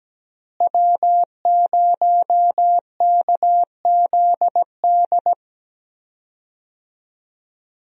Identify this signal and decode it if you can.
Morse code, heard as a keyed tone, about 17 wpm, 700 Hz. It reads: W0KZD
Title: USA Call Signs - 5 Characters - Repeated 2x - 17wpm